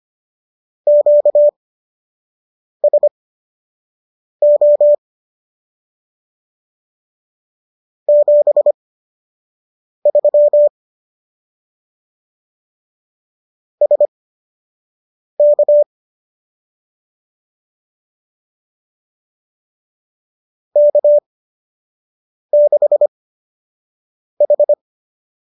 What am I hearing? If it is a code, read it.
Morse code, heard as a keyed tone, about 25 wpm, spaced as for 6 wpm, 590 Hz. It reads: QSO 73 SK K6H
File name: qso003 06